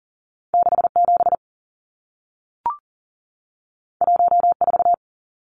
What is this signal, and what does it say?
Morse code, heard as a keyed tone, about 40 wpm, 700 Hz. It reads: 67 14
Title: Numbers - 2-Digit 40wpm